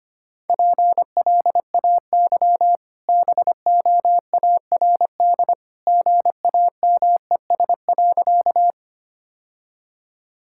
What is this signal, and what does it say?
Morse code, heard as a keyed tone, about 25 wpm, 690 Hz. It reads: PLAY BOARD GAMES.
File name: Easy Sentences with Hints - 25wpm